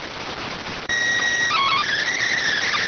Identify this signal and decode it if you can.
SSTV (slow-scan television), PD 290.